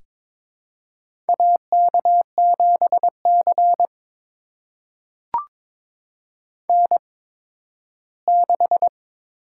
Morse code, heard as a keyed tone, about 23 wpm, spaced as for 13 wpm, 700 Hz. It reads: AK7C N6
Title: USA Call Signs - 4 Characters - 8x Character Spacing - 22wpm